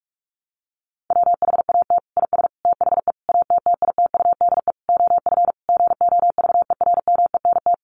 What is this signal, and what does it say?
Morse code, heard as a keyed tone, about 44 wpm, 700 Hz. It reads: WHAT IS THE ATTITUDE OF GOVERMENT
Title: Sentences from Top 1300 Words 45wpm